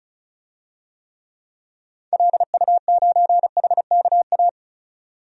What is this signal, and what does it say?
Morse code, heard as a keyed tone, about 34 wpm, 700 Hz. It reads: LU9HKA
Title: DXCC-100 Call Signs - 6 Characters - 35wpm